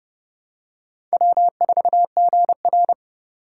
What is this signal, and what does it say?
Morse code, heard as a keyed tone, about 25 wpm, 700 Hz. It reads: W4GR